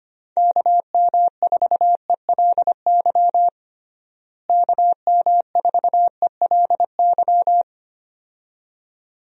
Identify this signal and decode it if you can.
Morse code, heard as a keyed tone, about 25 wpm, 700 Hz. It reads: KM4ELY KM4ELY
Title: USA Call Signs - 6 Characters - Repeated 2x - 25wpm